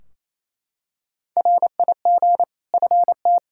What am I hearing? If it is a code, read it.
Morse code, heard as a keyed tone, about 28 wpm, 700 Hz. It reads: RIG FT
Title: Practice QSO - Exchange 2 - Info - 28wpm